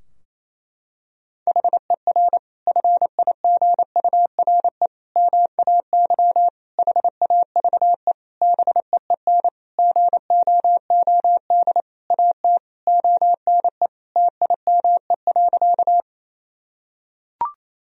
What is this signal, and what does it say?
Morse code, heard as a keyed tone, about 28 wpm, 700 Hz. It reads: HER FIGURE MAY HAVE BEEN GOOD AT ONE TIME.
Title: Sentences from Top 1000 Words - Review Part 2 - 28wpm